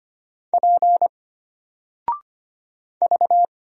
Morse code, heard as a keyed tone, about 25 wpm, 700 Hz. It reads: P V
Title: Single Letter-Number 25wpm